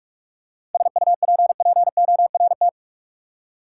Morse code, heard as a keyed tone, about 46 wpm, 700 Hz. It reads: SUPPORT
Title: Practice QSO - Exchange 3 - More Info - 45wpm